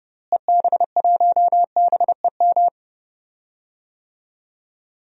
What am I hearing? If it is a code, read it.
Morse code, heard as a keyed tone, about 30 wpm, 700 Hz. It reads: EB1BEM
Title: DXCC-100 Call Signs - 6 Characters - Repeated 3x - 30wpm